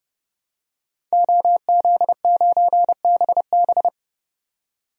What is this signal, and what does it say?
Morse code, heard as a keyed tone, about 31 wpm, 700 Hz. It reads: OZ9BB
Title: DXCC-100 Call Signs - 5 Characters - 30wpm